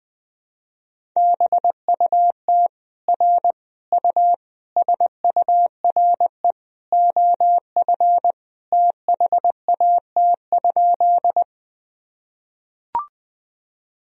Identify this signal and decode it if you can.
Morse code, heard as a keyed tone, about 20 wpm, 700 Hz. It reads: BUT R U SURE OF THAT?
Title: Sentences from Top 300 Words and Abbreviations 20wpm